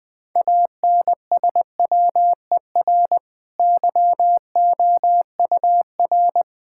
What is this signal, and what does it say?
Morse code, heard as a keyed tone, about 21 wpm, 700 Hz. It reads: ANSWER YOUR
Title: Easy Sentences with Hints - 20wpm